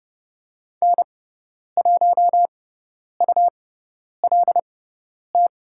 Morse code, heard as a keyed tone, about 30 wpm, spaced as for 10 wpm, 700 Hz. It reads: N1ULT
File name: Lesson 18g - Letter m - Call Signs 30wpm